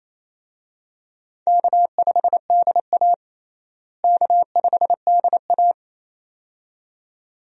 Morse code, heard as a keyed tone, about 28 wpm, 700 Hz. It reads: K5DA K5DA
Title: USA Call Signs - 4 Characters - Repeated 2x - 28wpm